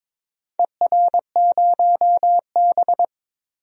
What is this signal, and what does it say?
Morse code, heard as a keyed tone, about 21 wpm, 700 Hz. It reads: ER0B
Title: DXCC-100 Call Signs - 4 Characters - Repeated 2x - 22wpm